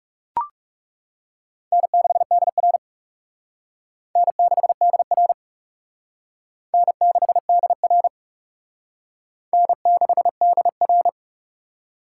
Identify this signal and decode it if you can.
Morse code, heard as a keyed tone, about 37 wpm, 700 Hz. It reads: N6DR N6DR N6DR N6DR
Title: Speed Racing - USA 1x2 Call Signs 30wpm